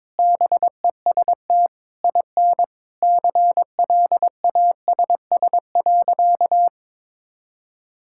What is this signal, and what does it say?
Morse code, heard as a keyed tone, about 22 wpm, 700 Hz. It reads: BEST IN CLASS.
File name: practice / 22wpm / Sentences with 8-Word Phrases - 22wpm